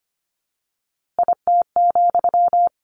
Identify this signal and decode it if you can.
Morse code, heard as a keyed tone, about 25 wpm, 700 Hz. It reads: IT,
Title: A Sound Of Thunder 25wpm